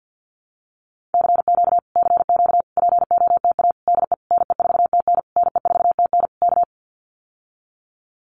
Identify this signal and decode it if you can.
Morse code, heard as a keyed tone, about 50 wpm, 700 Hz. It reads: CQ CQ POTA DE NE4TN NE4TN K
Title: Common QSO Phrases Encore 50wpm